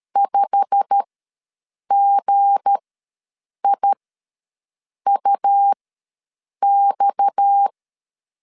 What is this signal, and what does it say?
Morse code, heard as a keyed tone, about 13 wpm, spaced as for 7 wpm, 800 Hz. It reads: 5GIUX
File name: LEZIONI / 9 2 ws 12wpm 3406